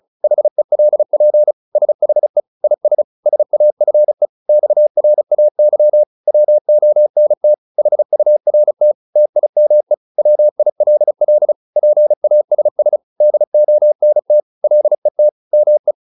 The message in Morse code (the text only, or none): HELP SHE IS SAFE XRAY WONT HURT TIME WILL PASS DONT LET ME